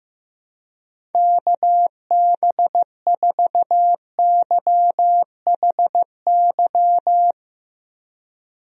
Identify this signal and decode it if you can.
Morse code, heard as a keyed tone, about 15 wpm, 700 Hz. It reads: KB4YHY
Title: USA Call Signs - 6 Characters - Repeated 3x - 15wpm